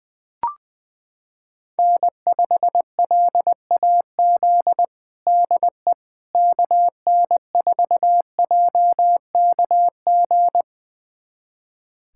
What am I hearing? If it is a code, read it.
Morse code, heard as a keyed tone, about 20 wpm, 700 Hz. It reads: N5LAZ DE KN4JKG
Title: USA Callsign DE Callsign 20wpm